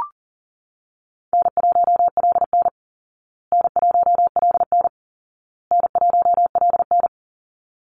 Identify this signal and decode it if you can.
Morse code, heard as a keyed tone, about 37 wpm, 700 Hz. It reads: N1LN N1LN N1LN
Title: Call Signs Repeated 2x - 40wpm